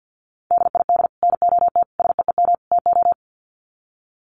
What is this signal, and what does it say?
Morse code, heard as a keyed tone, about 50 wpm, 700 Hz. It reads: DID NOT SEEM TO
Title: Sets of 4 Words 50wpm